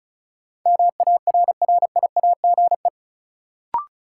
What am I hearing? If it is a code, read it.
Morse code, heard as a keyed tone, about 34 wpm, 700 Hz. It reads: MARRIAGE
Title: Sentences from Top 800 Words - New Word Drill - 35wpm